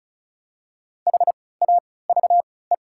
Morse code, heard as a keyed tone, about 36 wpm, spaced as for 20 wpm, 700 Hz. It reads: HAVE
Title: 3x Character Spacing - Top 300 Words - 35wpm